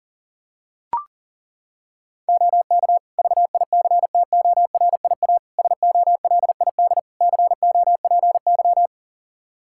Morse code, heard as a keyed tone, about 37 wpm, 700 Hz. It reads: OK VICTORIA SOLID COPY
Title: Practice QSO - Exchange 3 - More Info - 40wpm